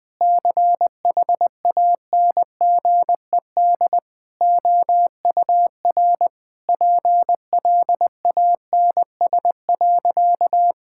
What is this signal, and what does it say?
Morse code, heard as a keyed tone, about 20 wpm, 700 Hz. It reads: CHANGED OUR PLANS.
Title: Sentences with 5-Word Phrases - Spoken Beginning - 20wpm